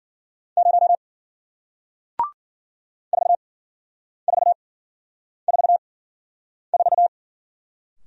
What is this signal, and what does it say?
Morse code, heard as a keyed tone, about 53 wpm, spaced as for 9 wpm, 700 Hz. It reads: 0 4444